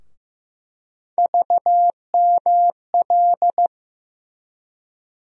Morse code, heard as a keyed tone, about 15 wpm, 700 Hz. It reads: VML
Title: USA Call Sign Postfixes - 3 Letters 15wpm